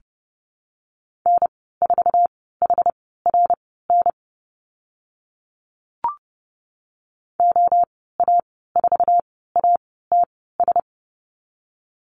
Morse code, heard as a keyed tone, about 30 wpm, spaced as for 17 wpm, 700 Hz. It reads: N4HRN OA4ATS
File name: Lesson 15g - Number 5 - Call Signs 30wpm